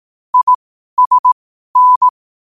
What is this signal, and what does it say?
Morse code, heard as a keyed tone, about 18 wpm, spaced as for 12 wpm, 1000 Hz. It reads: ISN
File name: qotdfiles / 12wpm